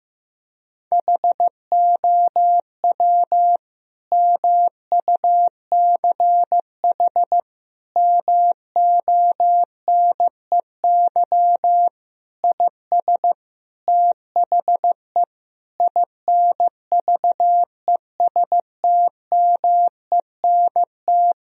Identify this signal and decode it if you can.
Morse code, heard as a keyed tone, about 15 wpm, 700 Hz. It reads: HOW MUCH MONEY IS THE INVESTMENT
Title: Sentences from Top 1000 Words - Review Part 4 - 15wpm